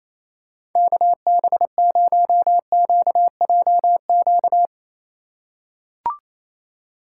Morse code, heard as a keyed tone, about 28 wpm, 700 Hz. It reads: KB0QJQ
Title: USA Call Signs - 6 Characters - 2x Character Spacing - 28wpm